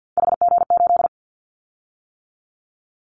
Morse code, heard as a keyed tone, about 50 wpm, 700 Hz. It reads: HG8
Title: Callsign Prefixes for Top 50 DXCC - 50wpm